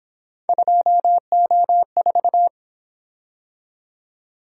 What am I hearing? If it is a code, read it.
Morse code, heard as a keyed tone, about 26 wpm, 700 Hz. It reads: 2O4
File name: Speed-Racing - Three Letter-Number Combinations - 22wpm